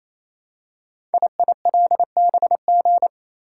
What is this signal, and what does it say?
Morse code, heard as a keyed tone, about 28 wpm, 700 Hz. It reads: IILBG
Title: Code Groups - 5 Letters - 28wpm